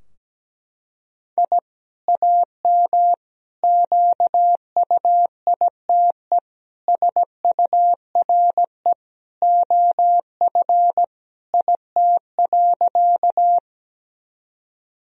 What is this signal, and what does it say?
Morse code, heard as a keyed tone, about 17 wpm, 700 Hz. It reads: I AM QUITE SURE OF IT.
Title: Sentences from Top 1000 Words - Review Part 2 - 17wpm